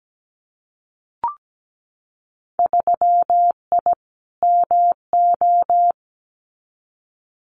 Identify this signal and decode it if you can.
Morse code, heard as a keyed tone, about 17 wpm, 700 Hz. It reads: 3I MO
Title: Winter Field Day Exchanges 17wpm